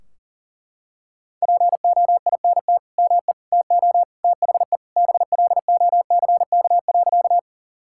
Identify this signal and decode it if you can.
Morse code, heard as a keyed tone, about 40 wpm, 700 Hz. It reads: POINT ME TO THE BLOCK.
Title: Sentences from Top 1600 Words 40wpm